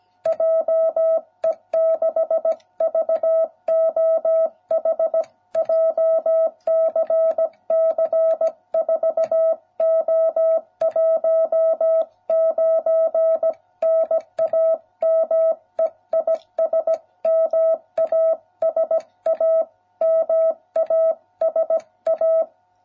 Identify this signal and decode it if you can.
Morse code, heard as a keyed tone, about 17 wpm, 660 Hz. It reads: JE6VOH JCC4O19 NAME IS MASA MASA